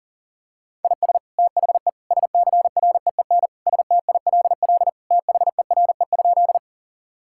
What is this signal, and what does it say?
Morse code, heard as a keyed tone, about 40 wpm, 700 Hz. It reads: IS THE SCREEN STILL THERE?
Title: Sentences from Top 1000 Words - Review Part 6 - 40wpm